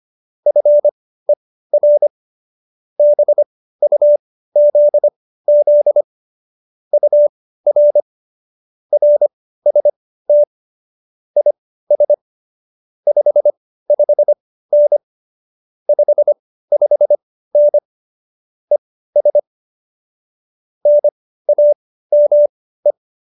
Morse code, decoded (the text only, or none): FER BUZZ UR RST IS 55N 55N ES NAME